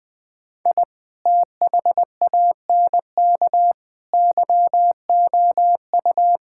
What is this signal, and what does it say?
Morse code, heard as a keyed tone, about 20 wpm, 700 Hz. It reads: I THANK YOU
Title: Sentences with 8-Word Phrases - Spoken Beginning - 20wpm